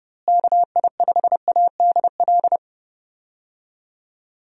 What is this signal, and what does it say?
Morse code, heard as a keyed tone, about 31 wpm, 700 Hz. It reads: KI5ADL
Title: USA Call Signs - 6 Characters - Repeated 3x - 30wpm